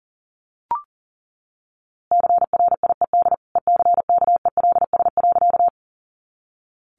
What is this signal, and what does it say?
Morse code, heard as a keyed tone, about 40 wpm, 700 Hz. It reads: CRIED ECKELS.
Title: A Sound Of Thunder 40wpm